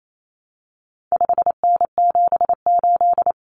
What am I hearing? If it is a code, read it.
Morse code, heard as a keyed tone, about 28 wpm, 700 Hz. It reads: 5N78
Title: Common Cut Numbers - 4-Digit 28wpm